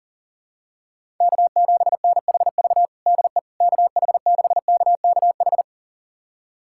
Morse code, heard as a keyed tone, about 40 wpm, 700 Hz. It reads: K7NHV DE KH6KKH
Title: USA Callsign DE Callsign 40wpm